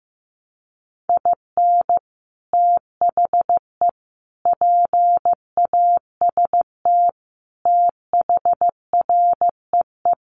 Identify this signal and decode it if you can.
Morse code, heard as a keyed tone, about 15 wpm, 700 Hz. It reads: IN THE PAST THREE
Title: Sets of 5 Words Encore2 15wpm